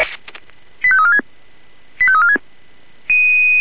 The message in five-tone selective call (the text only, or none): 85346 85346